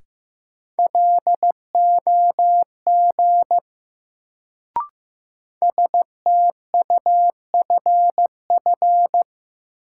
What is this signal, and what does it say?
Morse code, heard as a keyed tone, about 16 wpm, 700 Hz. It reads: LOG STUFF
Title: Sentences from Top 1400 Words - New Word Drill - 15wpm